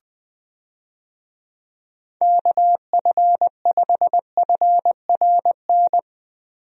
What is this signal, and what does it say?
Morse code, heard as a keyed tone, about 20 wpm, 700 Hz. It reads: KF5FRN
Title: USA Call Signs - 6 Characters - Repeated 2x - 20wpm